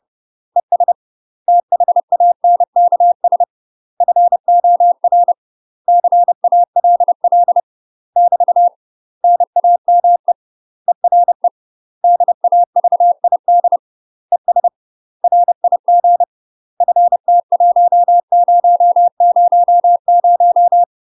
Morse code, heard as a keyed tone, about 30 wpm, 700 Hz. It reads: ES THANKS FOR CALL <BT> NAME ERE DAVID ES RIG FT1000
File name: UK MASTER QSOs TRAINING / 30WPM / UK MASTER QSOs 0024